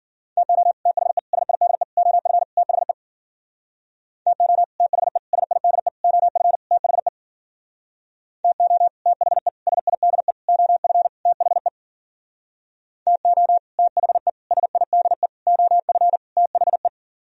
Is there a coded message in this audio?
Yes: Morse code from 0.4 to 16.9 s, heard as a keyed tone, about 53 wpm, 700 Hz. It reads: TO THE SIDE OF THE TO THE SIDE OF THE TO THE SIDE OF THE TO THE SIDE OF THE